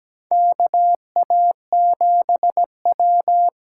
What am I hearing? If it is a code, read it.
Morse code, heard as a keyed tone, about 17 wpm, 700 Hz. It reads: KA7W
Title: USA Call Signs - 8x Character Spacing - 17wpm